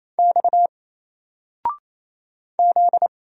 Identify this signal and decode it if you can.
Morse code, heard as a keyed tone, about 27 wpm, 700 Hz. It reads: X Z
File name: Single-Letter-Number-28wpm